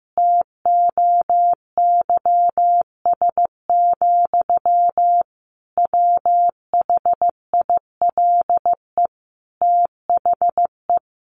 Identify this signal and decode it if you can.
Morse code, heard as a keyed tone, about 14 wpm, 700 Hz. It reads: TOYS, WHILE THE
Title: A Sound Of Thunder 15wpm